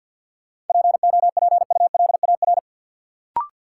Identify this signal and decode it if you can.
Morse code, heard as a keyed tone, about 46 wpm, 700 Hz. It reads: POPULAR
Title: Sentences from Top 1200 Words - New Word Drill Alt - 50wpm